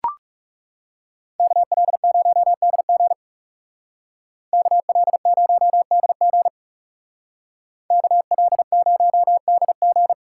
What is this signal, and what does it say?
Morse code, heard as a keyed tone, about 34 wpm, 700 Hz. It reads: KL0DG KL0DG KL0DG
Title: Speed Racing - USA 2x2 Call Signs 30wpm